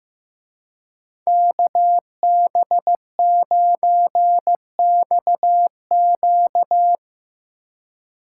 Morse code, heard as a keyed tone, about 17 wpm, 700 Hz. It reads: KB9XQ
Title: Speed Racing - USA 2x2 Call Signs 15wpm